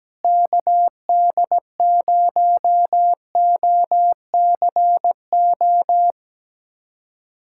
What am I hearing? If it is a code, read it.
Morse code, heard as a keyed tone, about 17 wpm, 700 Hz. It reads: KD0OCO
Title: USA Callsign DE Callsign - 6 Characters - 17wpm